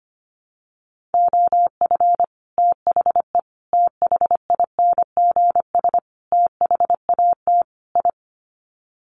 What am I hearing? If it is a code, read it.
Morse code, heard as a keyed tone, about 25 wpm, 700 Hz. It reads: OF THE THINGS THAT I
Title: Sets of 5 Words Encore 25wpm